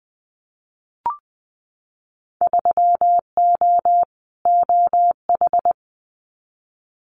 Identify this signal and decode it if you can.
Morse code, heard as a keyed tone, about 20 wpm, 700 Hz. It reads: 3O OH
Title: Winter Field Day Exchanges 20wpm